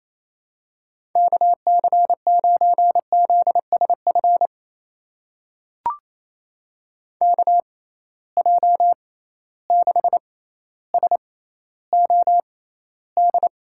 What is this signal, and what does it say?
Morse code, heard as a keyed tone, about 28 wpm, spaced as for 14 wpm, 700 Hz. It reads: KC9ZSF KJ6SOD
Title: USA Call Signs - 6 Characters - 6x Character Spacing - 28wpm